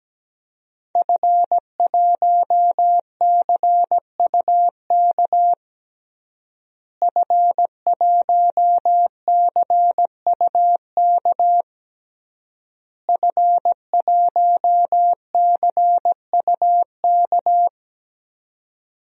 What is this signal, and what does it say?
Morse code, heard as a keyed tone, about 17 wpm, 700 Hz. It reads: F1CUK F1CUK F1CUK
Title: DXCC-100 Call Signs - 5 Characters - Repeated 3x - 17wpm